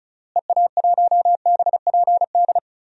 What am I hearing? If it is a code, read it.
Morse code, heard as a keyed tone, about 34 wpm, 700 Hz. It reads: EA1BPD
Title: DXCC-100 Call Signs - 6 Characters - Repeated 2x - 35wpm